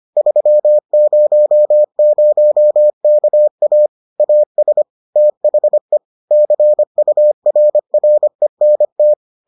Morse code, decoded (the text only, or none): 300KA AS THE CURRENT